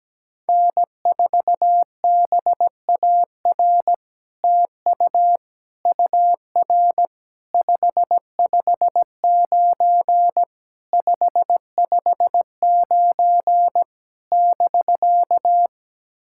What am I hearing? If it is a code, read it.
Morse code, heard as a keyed tone, about 17 wpm, 700 Hz. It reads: N4BAR TU UR 559 559 <BK>
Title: SOTA QSO Practice 17wpm